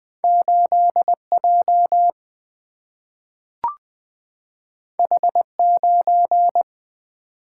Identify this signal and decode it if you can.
Morse code, heard as a keyed tone, about 20 wpm, 700 Hz. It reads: 8J H9
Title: Single-Letter Single-Number 20wpm